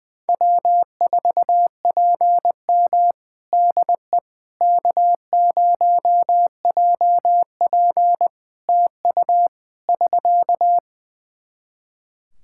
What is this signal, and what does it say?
Morse code, heard as a keyed tone, about 20 wpm, 700 Hz. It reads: W4PM DE K0JP TU <SK>
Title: Practice QSO - Exchange 4 - Ending - 20wpm